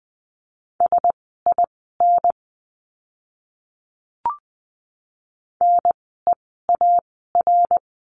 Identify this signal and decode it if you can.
Morse code, heard as a keyed tone, about 20 wpm, spaced as for 14 wpm, 700 Hz. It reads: SIN NEAR
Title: 054 Lesson 10f - Letter r - Words 20wpm